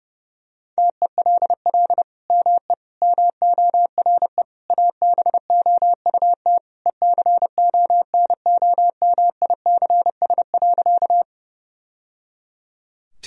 Morse code, heard as a keyed tone, about 32 wpm, 700 Hz. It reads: TELL ME MORE ABOUT ECONOMICS.
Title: Sentences from Top 1000 Words - Review Part 3 - 30wpm